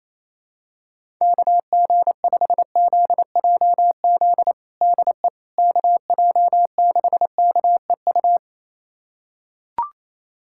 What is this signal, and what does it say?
Morse code, heard as a keyed tone, about 28 wpm, 700 Hz. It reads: KG5ZJZ DE KJ6KEU
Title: USA Callsign DE Callsign - 6 Characters - 28wpm